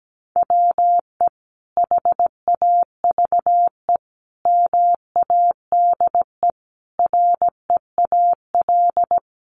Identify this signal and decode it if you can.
Morse code, heard as a keyed tone, about 17 wpm, 700 Hz. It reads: WE HAVE MADE REAL
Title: Sentences from Top 1400 Words 17wpm